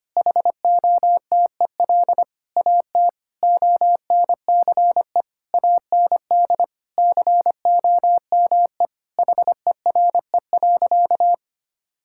Morse code, heard as a keyed tone, about 25 wpm, 700 Hz. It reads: HOTEL AT ONCE AND COME HERE.
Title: Sentences from Top 1600 Words 25wpm